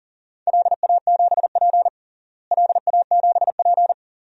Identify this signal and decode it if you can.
Morse code, heard as a keyed tone, about 40 wpm, 700 Hz. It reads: LA7P LA7P
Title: DXCC-100 Call Signs - 4 Characters - Repeated 2x - 40wpm